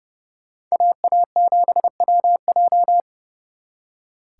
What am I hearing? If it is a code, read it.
Morse code, heard as a keyed tone, about 34 wpm, 700 Hz. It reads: AA7WJ
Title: Speed Racing - USA 2x2 Call Signs 30wpm